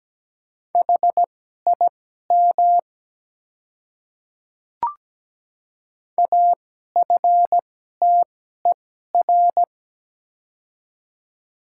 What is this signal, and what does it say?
Morse code, heard as a keyed tone, about 17 wpm, spaced as for 12 wpm, 700 Hz. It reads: HIM AFTER
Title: 2x Character Spacing - Top 200 Words - 17wpm